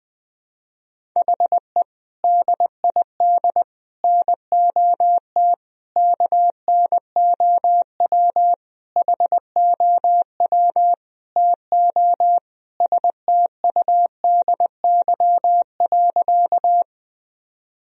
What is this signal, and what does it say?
Morse code, heard as a keyed tone, about 20 wpm, 700 Hz. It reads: HE DID NOT KNOW HOW TO STUDY.
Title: Sentences from Top 1000 Words - Review Part 1 - 20wpm